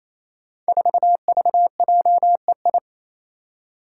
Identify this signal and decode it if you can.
Morse code, heard as a keyed tone, about 27 wpm, 700 Hz. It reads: 4VJEI
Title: Five Challenging Letter-Number Combinations 28wpm